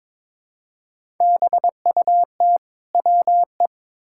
Morse code, heard as a keyed tone, about 21 wpm, 700 Hz. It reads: BUT WE
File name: Build A Phrase - 2-Word Phrases - 22wpm